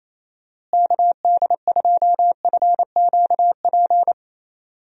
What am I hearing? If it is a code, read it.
Morse code, heard as a keyed tone, about 28 wpm, 700 Hz. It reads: KD2FQP
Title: USA Call Signs - 6 Characters - Repeated 3x - 28wpm